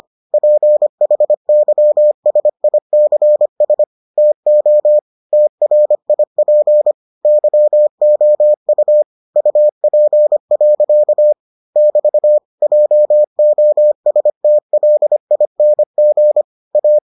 Morse code, heard as a keyed tone, about 25 wpm, 600 Hz. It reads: PHYSICS TO TRIP YOU UP. <BT> JOSTLING A